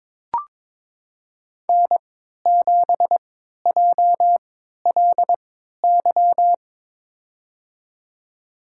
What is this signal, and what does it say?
Morse code, heard as a keyed tone, about 22 wpm, spaced as for 13 wpm, 700 Hz. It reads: N7JLY
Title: USA Call Signs - 5 Characters - 3x Character Spacing - 22wpm